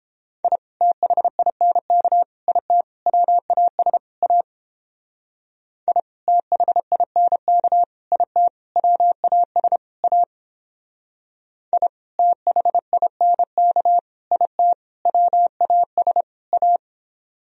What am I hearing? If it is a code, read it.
Morse code, heard as a keyed tone, about 31 wpm, 700 Hz. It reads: I THINK IT WAS A I THINK IT WAS A I THINK IT WAS A